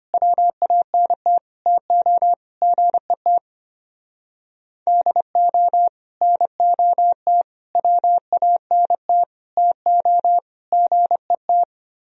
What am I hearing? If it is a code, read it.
Morse code, heard as a keyed tone, about 31 wpm, 700 Hz. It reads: WANT TO GET DO NOT WANT TO GET
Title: Speed-Racing - Sets of 5 Words 25wpm